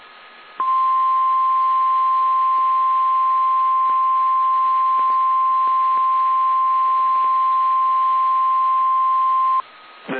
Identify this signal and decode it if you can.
Emergency Alert System (SAME) tones (the 1050 Hz attention tone). It is heard from 0.5 to 9.7 s.
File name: eas / KSDZKDJL / KSDZKDJL